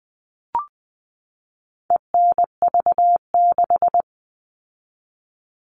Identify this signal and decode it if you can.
Morse code, heard as a keyed tone, about 20 wpm, 700 Hz. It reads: ENV6
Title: All Cut Numbers - 4-Digit 20wpm